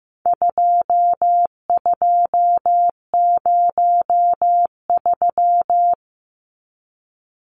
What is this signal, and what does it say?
Morse code, heard as a keyed tone, about 15 wpm, 700 Hz. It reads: 2203
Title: Numbers - 4-Digit 15wpm